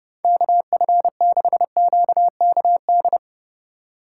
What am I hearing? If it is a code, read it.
Morse code, heard as a keyed tone, about 30 wpm, 700 Hz. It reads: KF6QKD
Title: USA Call Signs - 6 Characters - 4x Character Spacing - 30wpm